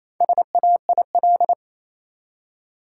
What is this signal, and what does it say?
Morse code, heard as a keyed tone, about 28 wpm, 700 Hz. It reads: SAIL
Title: Four Letter Words - 28wpm